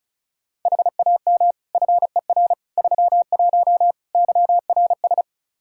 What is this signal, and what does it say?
Morse code, heard as a keyed tone, about 36 wpm, 700 Hz. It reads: HAM FER 31 YRS
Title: Practice QSO - Exchange 3 - More Info - 35wpm